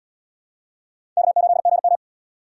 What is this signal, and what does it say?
Morse code, heard as a keyed tone, about 77 wpm, 700 Hz. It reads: K8GM